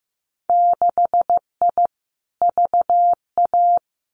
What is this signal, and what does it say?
Morse code, heard as a keyed tone, about 15 wpm, 700 Hz. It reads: 6I VA
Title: Winter Field Day Exchanges Encore 15wpm